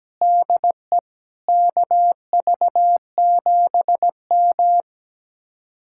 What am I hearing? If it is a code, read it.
Morse code, heard as a keyed tone, about 17 wpm, 700 Hz. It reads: DE KV7M
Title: Speed-Racing Common QSO Phrases Encore - 17wpm